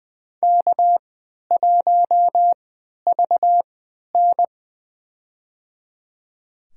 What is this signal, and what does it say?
Morse code, heard as a keyed tone, about 20 wpm, spaced as for 11 wpm, 700 Hz. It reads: K1VN
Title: USA Call Signs - 4 Characters - 3x Character Spacing - 20wpm